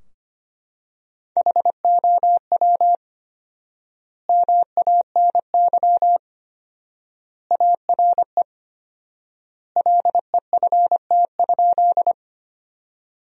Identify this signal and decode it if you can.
Morse code, heard as a keyed tone, about 25 wpm, 700 Hz. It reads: HOW MANY ARE LEFT?
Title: 4x Word-Spacing - Sentences from Top 300 Words - 25wpm